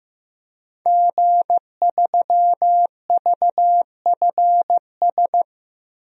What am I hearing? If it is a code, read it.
Morse code, heard as a keyed tone, about 15 wpm, 700 Hz. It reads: G3VFS
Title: DXCC-100 Call Signs - 5 Characters - 15wpm